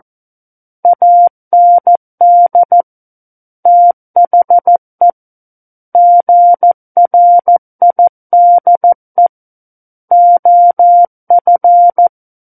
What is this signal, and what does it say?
Morse code, heard as a keyed tone, about 14 wpm, 700 Hz. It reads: AND THE GRIDE OF
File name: Book One / 05-Chapter 5 - 14 WPM